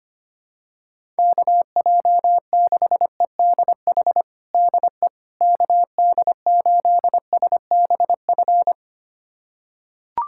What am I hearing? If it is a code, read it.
Morse code, heard as a keyed tone, about 25 wpm, 700 Hz. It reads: KJ6EDH DE KD8SBF
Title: USA Callsign DE Callsign - 6 Characters - 25wpm